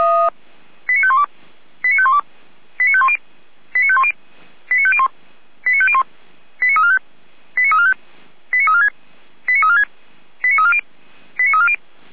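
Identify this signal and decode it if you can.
Five-tone selective call (ZVEI), heard as quick runs of five tones, tones of about 70 ms. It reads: 89512 89512 89510 89510 89501 89501 89345 89345 89346 89346 89340 89340
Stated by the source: ZVEI five-tone call received on SDR, from the Feuerwehr Orlamünde fire brigade audio archive.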